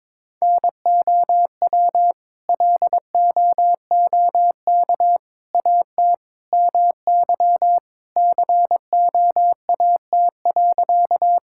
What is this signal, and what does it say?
Morse code, heard as a keyed tone, about 22 wpm, 700 Hz. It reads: NOW LOOK AT MY COAT.
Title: Sentences from Top 1600 Words 22wpm